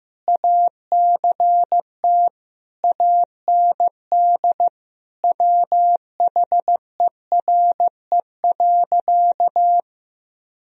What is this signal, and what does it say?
Morse code, heard as a keyed tone, about 15 wpm, 700 Hz. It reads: ACT AND WHERE.
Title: Sentences from Top 1000 Words - Review Part 1 - 15wpm